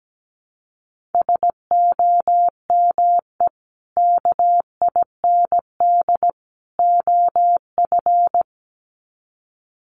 Morse code, heard as a keyed tone, about 17 wpm, 700 Hz. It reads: SOME KIND OF
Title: Sets of 3 Words 17wpm